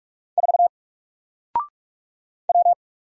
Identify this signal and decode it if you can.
Morse code, heard as a keyed tone, about 44 wpm, 700 Hz. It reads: 4 W
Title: Single Letter-Number Alt 45wpm